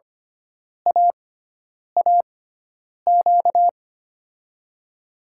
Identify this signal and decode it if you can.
Morse code, heard as a keyed tone, about 25 wpm, spaced as for 9 wpm, 700 Hz. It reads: AAQ
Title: USA Call Signs - 6x Character Spacing - 25wpm